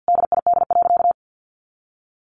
Morse code, heard as a keyed tone, about 50 wpm, 700 Hz. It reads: DID.
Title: Sentences from Top 1400 Words 50wpm